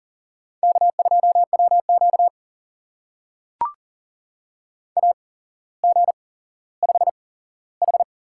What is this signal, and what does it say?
Morse code, heard as a keyed tone, about 40 wpm, spaced as for 17 wpm, 700 Hz. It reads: K2WQ AG5H
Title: USA Call Signs - 4 Characters - 8x Character Spacing - 40wpm